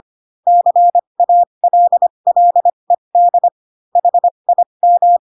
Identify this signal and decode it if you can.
Morse code, heard as a keyed tone, about 25 wpm, spaced as for 22 wpm, 700 Hz. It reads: CALLED HIM
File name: Pinocchio / Pinocchio ch2 18 25 ik1hge